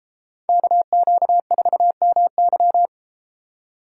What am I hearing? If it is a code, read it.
Morse code, heard as a keyed tone, about 32 wpm, 700 Hz. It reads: KQ4MY